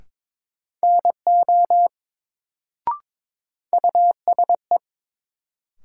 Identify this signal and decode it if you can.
Morse code, heard as a keyed tone, about 22 wpm, 700 Hz. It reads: NO USE
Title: Sentences from Top 100 Words - New Word Drill - 22wpm